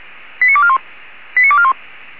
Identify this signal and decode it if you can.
Five-tone selective call (ZVEI), heard as quick runs of five tones, tones of about 70 ms. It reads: 89241 89241